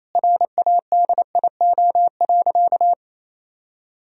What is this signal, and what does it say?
Morse code, heard as a keyed tone, about 28 wpm, 700 Hz. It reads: RADIO.
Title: Sentences from Top 1000 Words - Review Part 3 - 28wpm